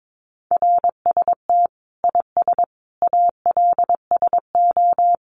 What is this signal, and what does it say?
Morse code, heard as a keyed tone, about 22 wpm, 700 Hz. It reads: RST IS ALSO
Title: Common QSO Phrases 22wpm